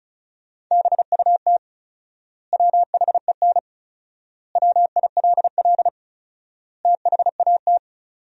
Morse code, heard as a keyed tone, about 36 wpm, 700 Hz. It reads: BUT WHEN WILL THAT
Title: 4x Word-Spacing - Sentences from Top 100 Words - 35wpm